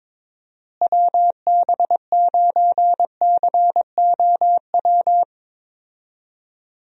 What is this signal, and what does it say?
Morse code, heard as a keyed tone, about 22 wpm, 700 Hz. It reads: WB9COW
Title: USA Call Signs - 6 Characters - Repeated 2x - 22wpm